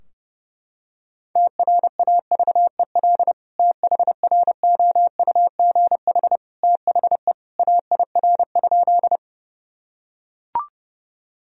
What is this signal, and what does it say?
Morse code, heard as a keyed tone, about 30 wpm, 700 Hz. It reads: TRAVEL THROUGH THE AIR?
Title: Sentences from Top 1000 Words - Review Part 2 - 30wpm